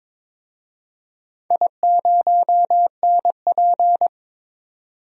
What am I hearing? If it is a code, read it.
Morse code, heard as a keyed tone, about 21 wpm, 700 Hz. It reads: I0NP
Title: DXCC-100 Call Signs - 4 Characters - 22wpm